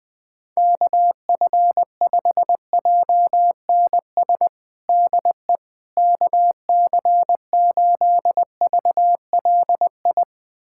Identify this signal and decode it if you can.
Morse code, heard as a keyed tone, about 21 wpm, 700 Hz. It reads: KF5JNS DE KC8VLI
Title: USA Callsign DE Callsign - 6 Characters - 20wpm